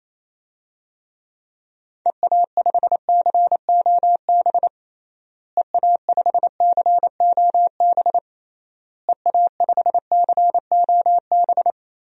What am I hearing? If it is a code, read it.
Morse code, heard as a keyed tone, about 28 wpm, 700 Hz. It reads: EA5COB EA5COB EA5COB
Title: DXCC-100 Call Signs - 6 Characters - Repeated 3x - 28wpm